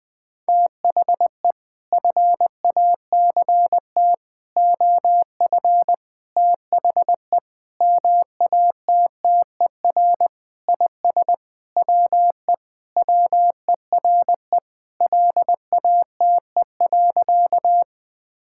Morse code, heard as a keyed tone, about 19 wpm, 700 Hz. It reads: THE FACT OF THE MATTER IS WE WERE LATE.
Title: Sentences with 5-Word Phrases - Spoken Beginning - 20wpm